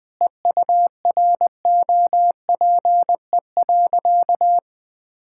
Morse code, heard as a keyed tone, about 20 wpm, 700 Hz. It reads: EUROPE.
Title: Sentences with 5-Word Phrases - 20wpm